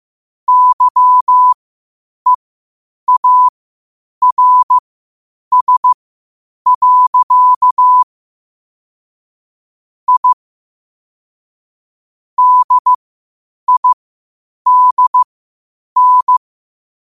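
Morse code, heard as a keyed tone, about 15 wpm, spaced as for 8 wpm, 1000 Hz. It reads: YEARS. I DIDN
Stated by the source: Morse code quotes at 8 WPM.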